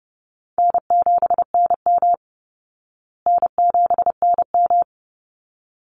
Morse code, heard as a keyed tone, about 30 wpm, 700 Hz. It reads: N7NM N7NM
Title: Easy Call Signs Repeated 2x - 30wpm